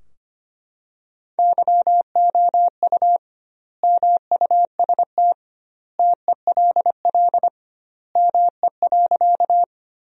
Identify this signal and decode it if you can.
Morse code, heard as a keyed tone, about 25 wpm, 700 Hz. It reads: YOU MUST TELL ME.
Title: 2x Word-Spacing - Sentences from Top 200 Words - 25wpm